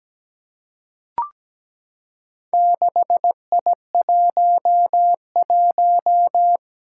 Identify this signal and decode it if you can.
Morse code, heard as a keyed tone, about 17 wpm, 700 Hz. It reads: 6I11
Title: Four Challenging Letter-Number Combinations 17wpm